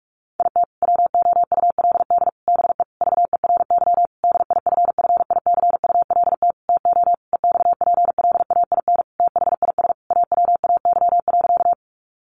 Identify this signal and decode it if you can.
Morse code, heard as a keyed tone, about 44 wpm, 700 Hz. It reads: IT WOULD BE VERY DIFFICULT TO EXPLAIN THIS AWAY.
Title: Sentences from Top 1100 Words 45wpm